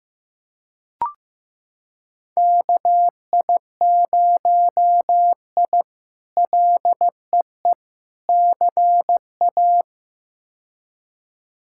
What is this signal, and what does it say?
Morse code, heard as a keyed tone, about 15 wpm, 700 Hz. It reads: KI0I LEE CA
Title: K1USN SST QSO Practice - 15wpm